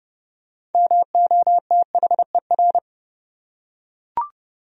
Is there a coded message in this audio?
Yes: Morse code from 0.7 to 2.8 s, heard as a keyed tone, about 28 wpm, 700 Hz. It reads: MOTHER